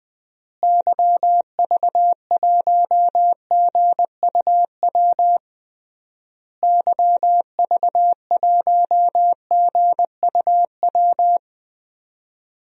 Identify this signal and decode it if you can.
Morse code, heard as a keyed tone, about 20 wpm, 700 Hz. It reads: YV1GUW YV1GUW
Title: DXCC-100 Call Signs - 6 Characters - Repeated 3x - 20wpm